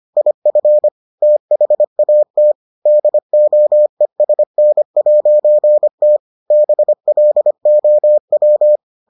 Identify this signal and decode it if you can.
Morse code, heard as a keyed tone, about 25 wpm, 600 Hz. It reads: IF THAT DOESN'T BLOW